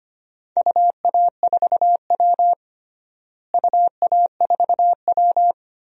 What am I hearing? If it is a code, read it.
Morse code, heard as a keyed tone, about 25 wpm, 700 Hz. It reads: UA4W UA4W
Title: DXCC-100 Call Signs - 4 Characters - Repeated 2x - 25wpm